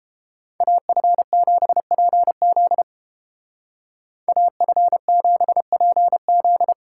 Morse code, heard as a keyed tone, about 29 wpm, 700 Hz. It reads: AF7PZ AF7PZ